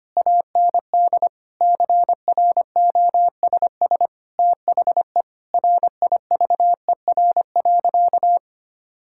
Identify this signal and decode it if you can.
Morse code, heard as a keyed tone, about 25 wpm, 700 Hz. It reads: AND CROSS THE RIVER.
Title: Sentences from Top 1000 Words - Review Part 6 - 25wpm